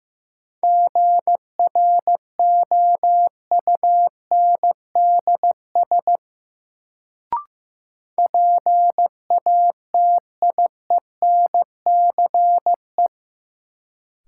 Morse code, heard as a keyed tone, about 15 wpm, 700 Hz. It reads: GROUNDS PATIENCE
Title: Sentences from Top 1300 Words - New Word Drill - 15wpm